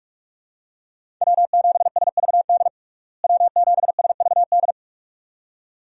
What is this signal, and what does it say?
Morse code, heard as a keyed tone, about 47 wpm, 700 Hz. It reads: W7SVD W7SVD
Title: USA Call Signs - 5 Characters - Repeated 2x - 45wpm